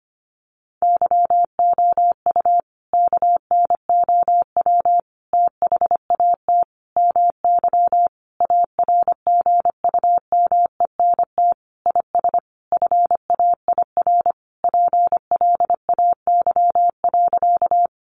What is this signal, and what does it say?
Morse code, heard as a keyed tone, about 25 wpm, 700 Hz. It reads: YOU KNOW THAT MY ARGUMENT IS FAIR PLAY.
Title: Sentences from Top 1400 Words 25wpm